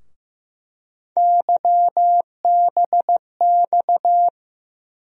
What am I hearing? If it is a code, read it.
Morse code, heard as a keyed tone, about 15 wpm, 700 Hz. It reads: YBX
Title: USA Call Sign Postfixes - 3 Letters 15wpm